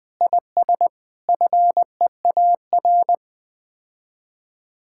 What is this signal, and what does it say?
Morse code, heard as a keyed tone, about 20 wpm, 700 Hz. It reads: IS FEAR
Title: Sets of 9 Words - 20wpm